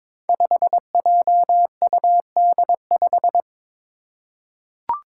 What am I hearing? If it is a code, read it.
Morse code, heard as a keyed tone, about 22 wpm, 700 Hz. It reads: 5JUD5
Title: Five Challenging Letter-Number Combinations 22wpm